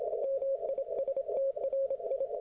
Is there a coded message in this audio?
Yes: Morse code from 0.2 to 2.3 s, heard as a keyed tone, about 26 wpm, 540 Hz. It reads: MISTRI